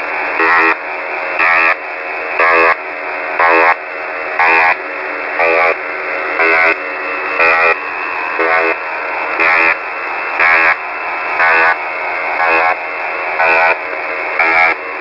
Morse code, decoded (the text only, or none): none